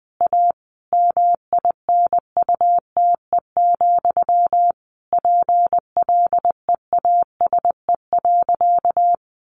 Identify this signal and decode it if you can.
Morse code, heard as a keyed tone, about 20 wpm, 700 Hz. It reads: A MINUTE, PLEASE.
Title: Sentences from Top 1500 Words 20wpm